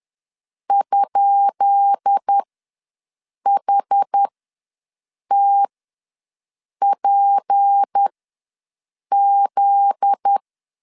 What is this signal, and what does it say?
Morse code, heard as a keyed tone, about 10 wpm, spaced as for 6 wpm, 800 Hz. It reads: ?HTPZ
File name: LEZIONI / 10 SIGNS / 10B INT ws 10wpm 4134